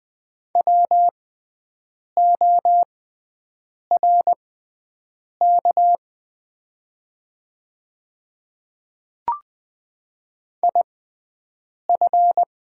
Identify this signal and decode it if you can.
Morse code, heard as a keyed tone, about 20 wpm, spaced as for 7 wpm, 700 Hz. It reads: WORK IF
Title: 6x Character Spacing - Top 500 Words - 20wpm